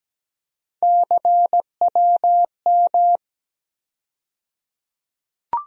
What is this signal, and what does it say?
Morse code, heard as a keyed tone, about 17 wpm, 700 Hz. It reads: CWM
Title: USA Call Sign Postfixes - 3 Letters 17wpm